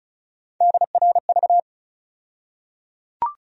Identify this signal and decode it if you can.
Morse code, heard as a keyed tone, about 36 wpm, 700 Hz. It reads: DRV
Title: USA Call Sign Postfixes - 3 Letters 35wpm